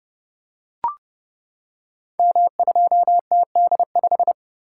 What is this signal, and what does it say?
Morse code, heard as a keyed tone, about 29 wpm, 700 Hz. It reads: M2TD5
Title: Code Groups - 5 Letter:Numbers - 30wpm